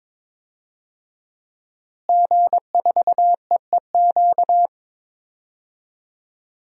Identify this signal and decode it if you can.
Morse code, heard as a keyed tone, about 21 wpm, 700 Hz. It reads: G4EEQ
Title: DXCC-100 Call Signs - 5 Characters - Repeated 2x - 22wpm